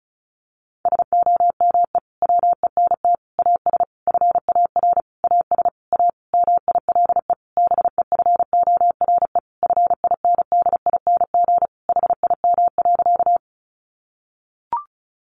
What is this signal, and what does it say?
Morse code, heard as a keyed tone, about 34 wpm, 700 Hz. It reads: SOME WENT AS FAR AS A MILE BEFORE FINDING HIM.
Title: Sentences from Top 1100 Words 35wpm